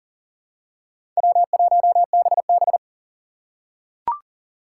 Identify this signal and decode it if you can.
Morse code, heard as a keyed tone, about 40 wpm, 700 Hz. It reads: W1BB
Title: USA Call Signs - 4 Characters - 2x Character Spacing - 40wpm